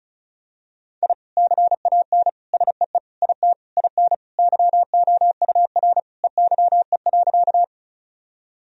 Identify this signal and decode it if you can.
Morse code, heard as a keyed tone, about 34 wpm, 700 Hz. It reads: I CAN SEE IT IN YOUR EYE.
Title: Sentences from Top 1000 Words - Review Part 1 - 35wpm